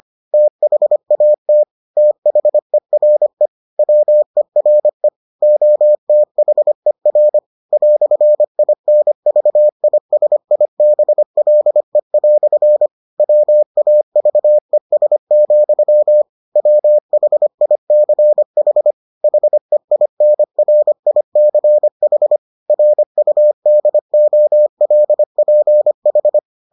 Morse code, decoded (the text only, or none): THAT THERE WERE OTHER "INVISIBLE" WAVES, WHICH HEINRICH RUDOLPH